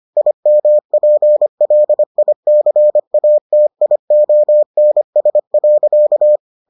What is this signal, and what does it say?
Morse code, heard as a keyed tone, about 25 wpm, 600 Hz. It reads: IMPLICATIONS.